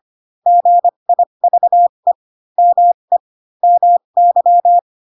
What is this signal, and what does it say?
Morse code, heard as a keyed tone, about 25 wpm, spaced as for 22 wpm, 700 Hz. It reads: GIVE ME MY
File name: Pinocchio / Pinocchio ch3 18 25 ik1hge